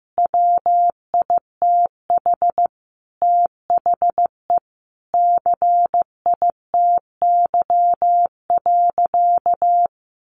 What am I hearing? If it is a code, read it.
Morse code, heard as a keyed tone, about 15 wpm, 700 Hz. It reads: WITH THE CITY.
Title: Sentences from Top 1500 Words 15wpm